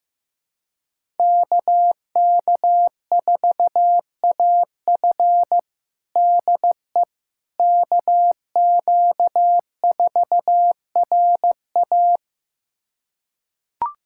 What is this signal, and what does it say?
Morse code, heard as a keyed tone, about 15 wpm, 700 Hz. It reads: KK4AF DE KQ4RA
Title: USA Callsign DE Callsign - 5 Characters - 15wpm